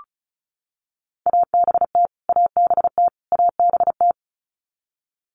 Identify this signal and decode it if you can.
Morse code, heard as a keyed tone, about 34 wpm, 700 Hz. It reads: ABT ABT ABT
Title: Abbreviations Repeated 3x - 35wpm